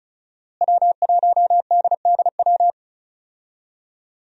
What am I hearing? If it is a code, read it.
Morse code, heard as a keyed tone, about 35 wpm, 700 Hz. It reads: W1DDW
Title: USA Call Signs - 5 Characters - Repeated 2x - 35wpm